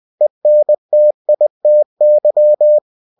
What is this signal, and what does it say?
Morse code, heard as a keyed tone, about 20 wpm, 600 Hz. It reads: ENTITY
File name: cw-books / winnie-the-pooh / winnie-20wpm-page-0113